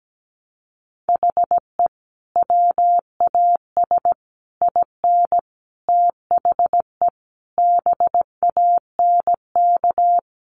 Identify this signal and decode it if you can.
Morse code, heard as a keyed tone, about 17 wpm, 700 Hz. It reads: HE WAS IN THE BANK
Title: Sentences from Top 900 Words 17wpm